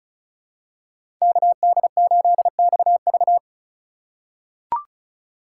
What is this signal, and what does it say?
Morse code, heard as a keyed tone, about 36 wpm, 700 Hz. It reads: KD8XV
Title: USA Call Signs - 5 Characters - 3x Character Spacing - 35wpm